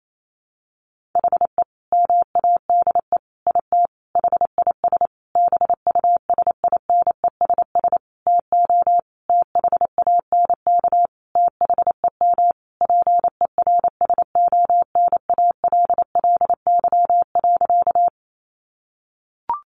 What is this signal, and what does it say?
Morse code, heard as a keyed tone, about 28 wpm, 700 Hz. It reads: HE MADE IT HIS BUSINESS TO THANK THEM PERSONALLY.
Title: Sentences from Top 1400 Words 28wpm